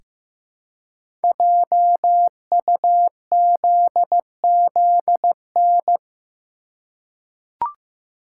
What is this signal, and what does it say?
Morse code, heard as a keyed tone, about 15 wpm, 700 Hz. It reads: JUZZN
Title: Code Groups - 5 Letters - 15wpm